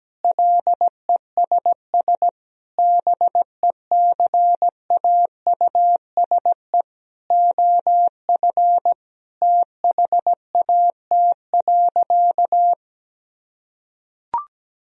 Morse code, heard as a keyed tone, about 17 wpm, 700 Hz. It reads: LESS BECAUSE OF THAT.
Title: Sentences from Top 1000 Words - Review Part 2 - 17wpm